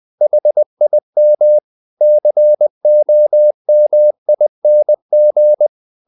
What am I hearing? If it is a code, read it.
Morse code, heard as a keyed tone, about 20 wpm, 600 Hz. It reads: HIM COMING
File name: the old man and the sea part 2 20wpm